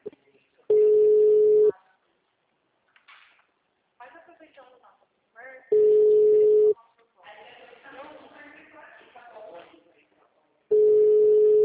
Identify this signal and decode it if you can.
Ringback tone (Europe, 425 Hz), with speech.